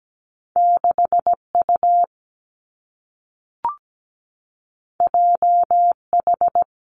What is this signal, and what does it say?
Morse code, heard as a keyed tone, about 17 wpm, 700 Hz. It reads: 6U JH
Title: Two Challenging Letter-Number Combinations 17wpm